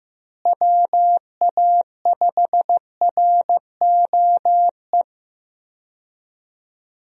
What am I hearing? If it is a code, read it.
Morse code, heard as a keyed tone, about 15 wpm, 700 Hz. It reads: WA5ROE
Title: USA Call Signs - 6 Characters - Repeated 3x - 15wpm